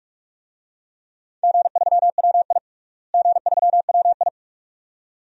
Morse code, heard as a keyed tone, about 46 wpm, 700 Hz. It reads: G3WI G3WI
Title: DXCC-100 Call Signs - 4 Characters - Repeated 2x - 45wpm